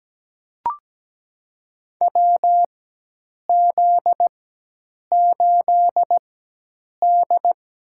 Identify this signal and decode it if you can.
Morse code, heard as a keyed tone, about 17 wpm, spaced as for 8 wpm, 700 Hz. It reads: WZ8D
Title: USA Call Signs - 4 Characters - 4x Character Spacing - 17wpm